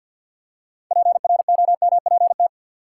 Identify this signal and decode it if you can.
Morse code, heard as a keyed tone, about 53 wpm, 700 Hz. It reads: PROMPT
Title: Six Letter Words - 50wpm